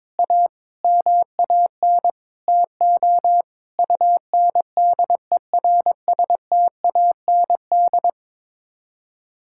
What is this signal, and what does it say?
Morse code, heard as a keyed tone, about 22 wpm, 700 Hz. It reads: A MAN TO UNDERSTAND
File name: Sets of 9 Words - 22wpm